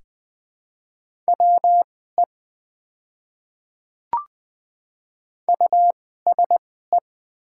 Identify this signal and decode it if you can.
Morse code, heard as a keyed tone, about 20 wpm, spaced as for 14 wpm, 700 Hz. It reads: WE USE
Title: 2x Character Spacing - Top 100 Words - 20wpm